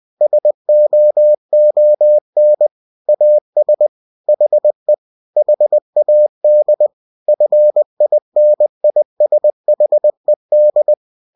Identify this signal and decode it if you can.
Morse code, heard as a keyed tone, about 20 wpm, 600 Hz. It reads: SOON AS HE HAD FINISHED
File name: cw-books / winnie-the-pooh / winnie-20wpm-page-0084